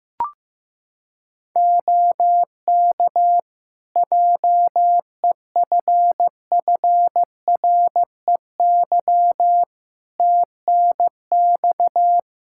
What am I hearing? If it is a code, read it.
Morse code, heard as a keyed tone, about 14 wpm, 700 Hz. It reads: OK JEFFREY TNX
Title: Practice QSO - Exchange 4 - Ending - 15wpm